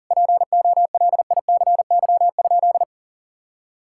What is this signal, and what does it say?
Morse code, heard as a keyed tone, about 40 wpm, 700 Hz. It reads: POLICY?
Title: Sentences from Top 1000 Words - Review Part 5 - 40wpm